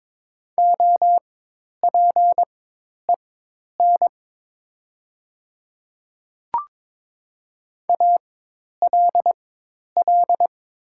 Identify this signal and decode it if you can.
Morse code, heard as a keyed tone, about 22 wpm, spaced as for 10 wpm, 700 Hz. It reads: OPEN ALL
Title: 4x Character Spacing - Top 300 Words - 22wpm